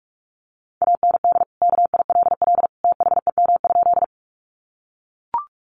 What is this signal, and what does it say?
Morse code, heard as a keyed tone, about 44 wpm, 700 Hz. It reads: AND KILL THEM?
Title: Sentences from Top 1400 Words 45wpm